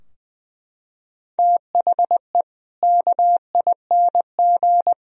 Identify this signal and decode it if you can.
Morse code, heard as a keyed tone, about 21 wpm, 700 Hz. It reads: THE KING
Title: Sentences from Top 1000 Words - Review Part 3 - 20wpm